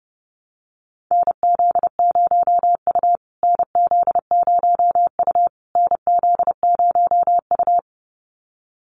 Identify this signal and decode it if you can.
Morse code, heard as a keyed tone, about 28 wpm, 700 Hz. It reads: NZ0U NZ0U NZ0U
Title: USA Call Signs Repeated 3x 30wpm